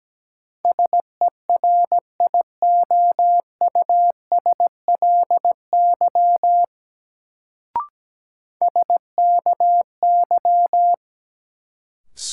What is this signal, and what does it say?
Morse code, heard as a keyed tone, about 17 wpm, 700 Hz. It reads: SERIOUSLY SKY
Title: Sentences from Top 1200 Words - New Word Drill - 17wpm